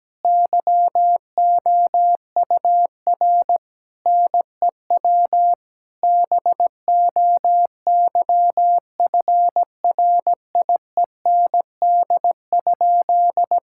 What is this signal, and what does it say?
Morse code, heard as a keyed tone, about 17 wpm, 700 Hz. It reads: YOUR NEW BOYFRIEND?
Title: Sentences from Top 1000 Words - Review Part 5 - 17wpm